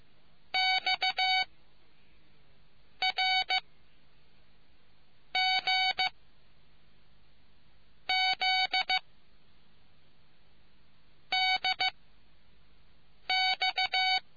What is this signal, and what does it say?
Morse code, heard as a keyed tone, about 15 wpm, 750 Hz. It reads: X R G Z D X